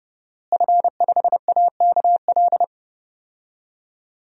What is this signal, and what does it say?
Morse code, heard as a keyed tone, about 31 wpm, 700 Hz. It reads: F5AKL
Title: DXCC-100 Call Signs - 5 Characters - Repeated 2x - 30wpm